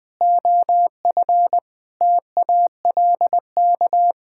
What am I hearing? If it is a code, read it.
Morse code, heard as a keyed tone, about 21 wpm, 700 Hz. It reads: OF TALK
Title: Sentences with 8-Word Phrases - Spoken Beginning - 20wpm